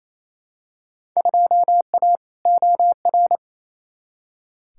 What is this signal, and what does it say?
Morse code, heard as a keyed tone, about 28 wpm, 700 Hz. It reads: 2A OR
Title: Field Day Exchanges Encore - 28wpm